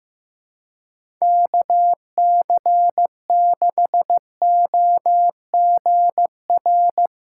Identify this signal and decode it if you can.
Morse code, heard as a keyed tone, about 16 wpm, 700 Hz. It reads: KC6OGR
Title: USA Callsign DE Callsign 15wpm